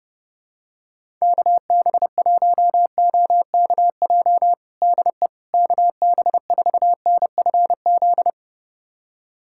Morse code, heard as a keyed tone, about 30 wpm, 700 Hz. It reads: KB1OKJ DE KB4NFZ
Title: USA Callsign DE Callsign 30wpm